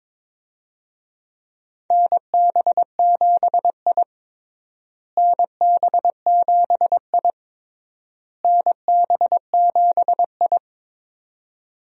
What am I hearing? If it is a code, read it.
Morse code, heard as a keyed tone, about 22 wpm, 700 Hz. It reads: NB7I NB7I NB7I
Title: USA Call Signs - 4 Characters - Repeated 3x - 22wpm